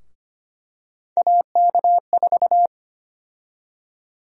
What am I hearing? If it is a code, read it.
Morse code, heard as a keyed tone, about 25 wpm, 700 Hz. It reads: AK4
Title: USA Call Sign Prefixes - Repeated 2x - 25wpm